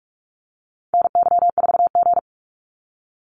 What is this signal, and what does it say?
Morse code, heard as a keyed tone, about 46 wpm, 700 Hz. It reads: NY4G
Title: NA SOTA Activator Callsigns 45wpm